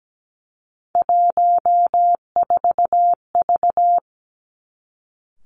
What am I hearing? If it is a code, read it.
Morse code, heard as a keyed tone, about 17 wpm, 700 Hz. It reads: 14V
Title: Three Challenging Letter-Number Combinations 17wpm